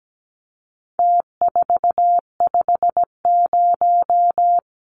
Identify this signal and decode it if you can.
Morse code, heard as a keyed tone, about 17 wpm, 700 Hz. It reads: T450
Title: Numbers - Contest Style - 4-Digit 17wpm